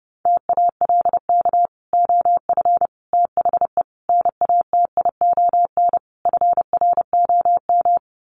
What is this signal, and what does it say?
Morse code, heard as a keyed tone, about 30 wpm, 700 Hz. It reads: TALK OF THE NATION FROM
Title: Sets of 5 Words 30wpm